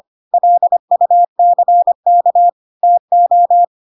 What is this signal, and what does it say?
Morse code, heard as a keyed tone, about 25 wpm, 700 Hz. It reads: LUCK TO
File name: Pinocchio / Pinocchio ch2 25 25 ik1hge